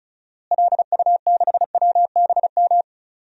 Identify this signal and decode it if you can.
Morse code, heard as a keyed tone, about 34 wpm, 700 Hz. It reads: LU6WBM
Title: DXCC-100 Call Signs - 6 Characters - Repeated 2x - 35wpm